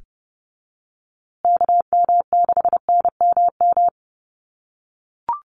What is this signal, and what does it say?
Morse code, heard as a keyed tone, about 30 wpm, 700 Hz. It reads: KM6NMM
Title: USA Call Signs Repeated 2x 30wpm